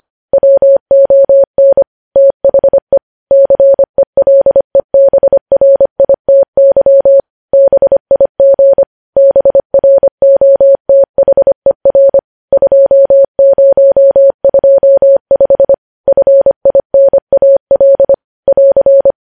Morse code, heard as a keyed tone, about 25 wpm, 550 Hz. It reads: WON THE CELEBRITY BIG BROTHER 2025 FINAL <AR>
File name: podcast / News headlines-1x-25x25